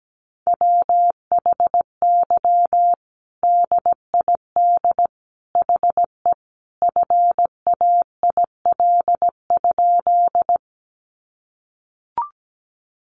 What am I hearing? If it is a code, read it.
Morse code, heard as a keyed tone, about 17 wpm, 700 Hz. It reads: WHY DID HE FAIL?
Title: Sentences from Top 1100 Words 17wpm